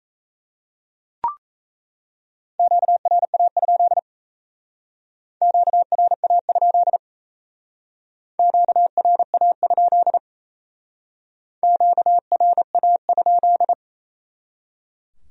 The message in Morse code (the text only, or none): QRA? QRA? QRA? QRA?